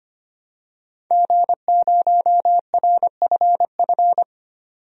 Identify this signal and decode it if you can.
Morse code, heard as a keyed tone, about 25 wpm, 700 Hz. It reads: G0RFF
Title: DXCC-100 Call Signs - 5 Characters - 25wpm